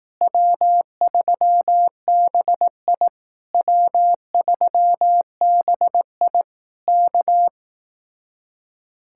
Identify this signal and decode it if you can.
Morse code, heard as a keyed tone, about 18 wpm, 700 Hz. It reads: W3BI W3BI K
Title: Speed-Racing Common QSO Phrases Encore - 15wpm